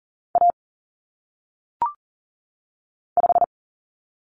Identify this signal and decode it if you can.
Morse code, heard as a keyed tone, about 40 wpm, 700 Hz. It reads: A 5
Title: Common Cut Numbers - 1-Digit 40wpm